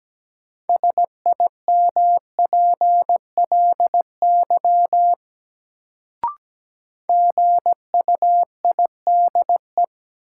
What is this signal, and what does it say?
Morse code, heard as a keyed tone, about 17 wpm, 700 Hz. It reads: SIMPLY GUIDE
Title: Sentences from Top 600 Words - New Word Drill - 17wpm